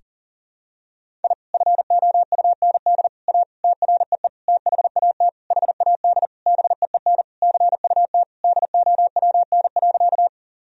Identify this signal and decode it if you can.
Morse code, heard as a keyed tone, about 40 wpm, 700 Hz. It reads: I FOUND A TREE THAT HAD BEEN CUT DOWN.
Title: Sentences from Top 1000 Words - Review Part 1 - 40wpm